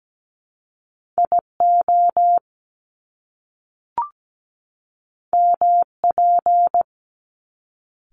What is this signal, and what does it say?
Morse code, heard as a keyed tone, about 17 wpm, 700 Hz. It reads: IO MP
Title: Top Two Letter Combinations in Words 17wpm